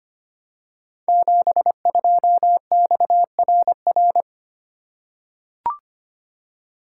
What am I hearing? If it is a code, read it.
Morse code, heard as a keyed tone, about 25 wpm, 700 Hz. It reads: 72XRR
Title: Code Groups - 5 Letter:Numbers - 25wpm